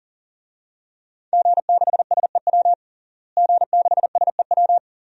Morse code, heard as a keyed tone, about 40 wpm, 700 Hz. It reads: G6SEW G6SEW
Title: DXCC-100 Call Signs - 5 Characters - Repeated 2x - 40wpm